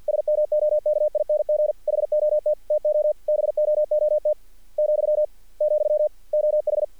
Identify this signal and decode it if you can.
Morse code, heard as a keyed tone, about 46 wpm, 600 Hz. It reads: SMOKING HOT TO BOOT , , OH
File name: archived / CRSnet